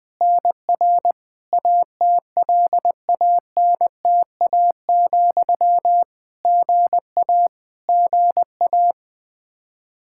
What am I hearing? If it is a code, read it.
Morse code, heard as a keyed tone, about 20 wpm, 700 Hz. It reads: NR ATLANTA, GA GA
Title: Practice QSO - Exchange 1 - Beginning - 20wpm